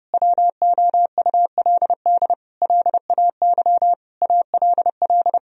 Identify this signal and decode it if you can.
Morse code, heard as a keyed tone, about 30 wpm, 700 Hz. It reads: WOULD LAY ALL
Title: Sentences from Top 1000 Words - Review Part 2 - 30wpm